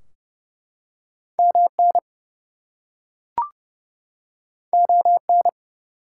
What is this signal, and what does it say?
Morse code, heard as a keyed tone, about 30 wpm, 700 Hz. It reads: MN ON
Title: US States and Canadian Provinces 30wpm